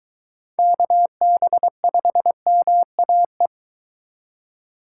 Morse code, heard as a keyed tone, about 23 wpm, 700 Hz. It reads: KB5MAE